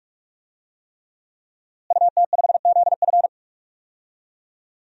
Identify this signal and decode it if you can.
Morse code, heard as a keyed tone, about 41 wpm, 700 Hz. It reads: UT5ZF
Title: DXCC-100 Call Signs - 5 Characters - Repeated 3x - 45wpm